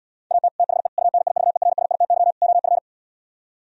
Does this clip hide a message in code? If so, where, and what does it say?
Yes: Morse code from 0.3 to 2.8 s, heard as a keyed tone, about 77 wpm, 700 Hz. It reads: AT THE UNIVERSITY OF